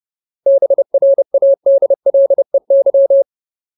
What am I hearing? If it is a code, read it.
Morse code, heard as a keyed tone, about 30 wpm, 550 Hz. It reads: BRADLEY
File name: Ultimate Word List / 30WPM Ultimate